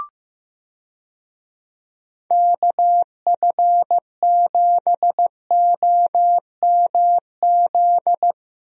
Morse code, heard as a keyed tone, about 15 wpm, 700 Hz. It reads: KF7OMZ
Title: USA Call Signs - 6 Characters - Repeated 2x - 15wpm